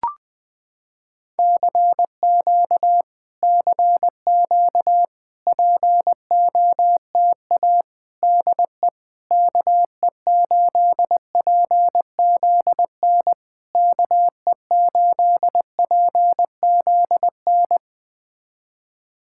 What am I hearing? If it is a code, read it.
Morse code, heard as a keyed tone, about 20 wpm, 700 Hz. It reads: CQ CQ POTA DE KE8PZN KE8PZN
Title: POTA QSO Practice - 20wpm